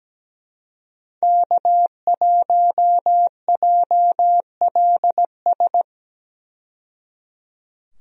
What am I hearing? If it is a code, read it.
Morse code, heard as a keyed tone, about 17 wpm, 700 Hz. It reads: K1JLS
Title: USA Call Signs - 5 Characters - 17wpm